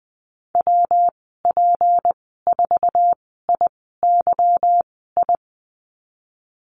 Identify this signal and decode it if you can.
Morse code, heard as a keyed tone, about 20 wpm, spaced as for 15 wpm, 700 Hz. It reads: WP4IYI
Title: 179 Lesson 29g - Slash - Call Signs 20wpm